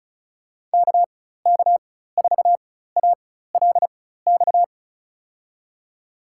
Morse code, heard as a keyed tone, about 34 wpm, spaced as for 16 wpm, 700 Hz. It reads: KK4ALX
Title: USA Call Signs - 6 Characters - 4x Character Spacing - 35wpm